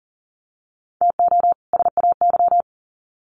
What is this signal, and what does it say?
Morse code, heard as a keyed tone, about 40 wpm, 700 Hz. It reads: TO SAY
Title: Sets of 2 Words 40wpm